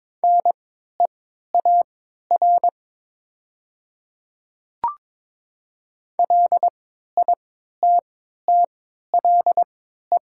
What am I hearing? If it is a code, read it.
Morse code, heard as a keyed tone, about 22 wpm, spaced as for 12 wpm, 700 Hz. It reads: NEAR LITTLE
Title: 3x Character Spacing - Top 300 Words - 22wpm